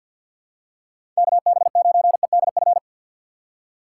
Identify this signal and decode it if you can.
Morse code, heard as a keyed tone, about 51 wpm, 700 Hz. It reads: KB9EDF
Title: USA Call Signs - 6x Character Spacing - 50wpm